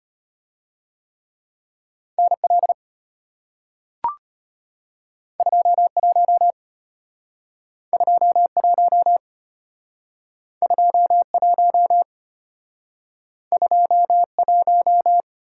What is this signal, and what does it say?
Morse code, heard as a keyed tone, about 34 wpm, 700 Hz. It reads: NL 21 21 21 21